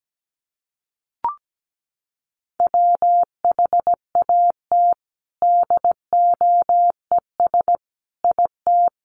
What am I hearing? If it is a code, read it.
Morse code, heard as a keyed tone, about 17 wpm, 700 Hz. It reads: WHAT DOES IT
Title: Sentences from Top 200 Words 17wpm